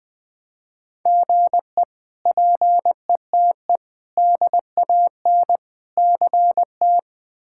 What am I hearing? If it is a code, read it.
Morse code, heard as a keyed tone, about 20 wpm, 700 Hz. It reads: GE PETE DAN CT
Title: K1USN SST QSO Practice - 20wpm